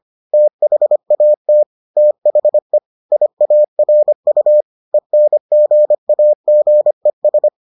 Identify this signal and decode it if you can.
Morse code, heard as a keyed tone, about 25 wpm, 600 Hz. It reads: THAT THE IARU ENGAGES